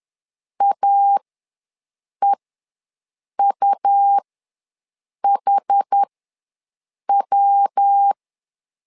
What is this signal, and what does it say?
Morse code, heard as a keyed tone, about 10 wpm, spaced as for 6 wpm, 800 Hz. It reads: AEUHW
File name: LEZIONI / 3 MWRO / 3 MWR ws 10wpm 2151